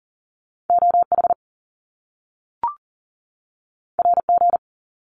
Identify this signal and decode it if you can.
Morse code, heard as a keyed tone, about 38 wpm, 700 Hz. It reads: OH RG
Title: Top Two Letter Combinations in Words 40wpm